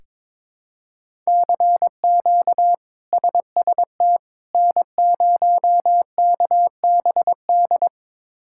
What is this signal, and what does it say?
Morse code, heard as a keyed tone, about 22 wpm, 700 Hz. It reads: CQ SST N0KBD
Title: K1USN SST QSO Practice - 22wpm